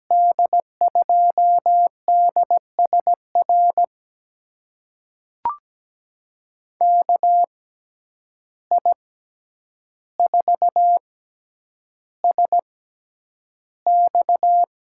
Morse code, heard as a keyed tone, about 17 wpm, spaced as for 9 wpm, 700 Hz. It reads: D2DSR KI4SX
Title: USA Call Signs - 6x Character Spacing - 17wpm